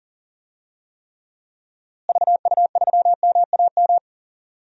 Morse code, heard as a keyed tone, about 40 wpm, 700 Hz. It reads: VU3MAM
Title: DXCC-100 Call Signs - 6 Characters - Repeated 2x - 40wpm